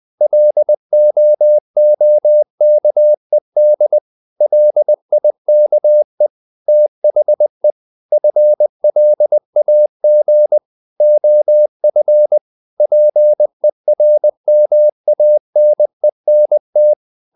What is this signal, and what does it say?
Morse code, heard as a keyed tone, about 20 wpm, 600 Hz. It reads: LOOKED LIKE THE FLAG OF PERMANENT
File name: the old man and the sea / the old man and the sea part 1 20wpm